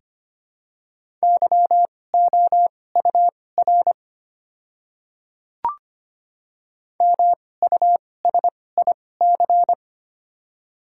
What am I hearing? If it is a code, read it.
Morse code, heard as a keyed tone, about 25 wpm, spaced as for 18 wpm, 700 Hz. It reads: YOUR MUSIC
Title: 2x Character Spacing - Top 500 Words - 25wpm